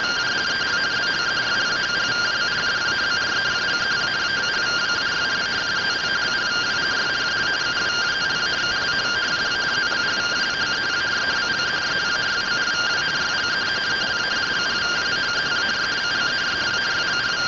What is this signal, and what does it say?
Two-tone teleprinter signal, about 170 Hz shift.